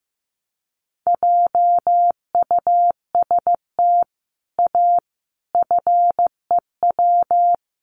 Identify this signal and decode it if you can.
Morse code, heard as a keyed tone, about 14 wpm, 700 Hz. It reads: JUST A FEW
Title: Sets of 5 Words Encore 15wpm